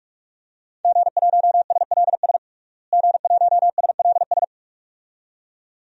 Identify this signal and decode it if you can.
Morse code, heard as a keyed tone, about 46 wpm, 700 Hz. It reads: G1SLS G1SLS
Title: DXCC-100 Call Signs - 5 Characters - Repeated 2x - 45wpm